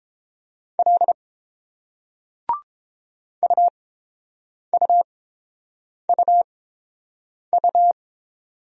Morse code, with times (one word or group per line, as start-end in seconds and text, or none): 0.8-1.1 L
3.4-7.9 UUUU